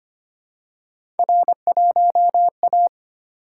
Morse code, heard as a keyed tone, about 25 wpm, 700 Hz. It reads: R1A
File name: Three Letter-Number Combinations - 25wpm